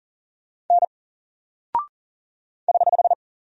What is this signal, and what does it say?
Morse code, heard as a keyed tone, about 39 wpm, 700 Hz. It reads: N <HH>
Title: Single Character (Letter:Number:Punctuation:Prosign) 40wpm